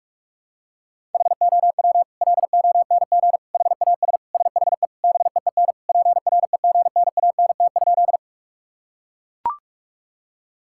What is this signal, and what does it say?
Morse code, heard as a keyed tone, about 42 wpm, 700 Hz. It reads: HOW LONG HAS SHE BEEN PREGNANT?
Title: Sentences from Top 1000 Words - Review Part 6 - 45wpm